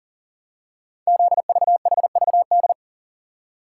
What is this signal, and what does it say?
Morse code, heard as a keyed tone, about 38 wpm, 690 Hz. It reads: ZVHVD
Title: Code Groups - 5 Letters - 40wpm